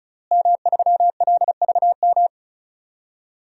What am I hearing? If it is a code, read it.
Morse code, heard as a keyed tone, about 34 wpm, 700 Hz. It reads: M3LVM
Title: DXCC-100 Call Signs - 5 Characters - Repeated 2x - 35wpm